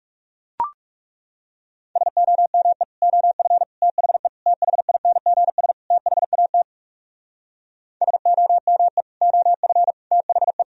Morse code, heard as a keyed tone, about 40 wpm, 700 Hz. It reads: SOME OF THE THINGS THAT SOME OF THE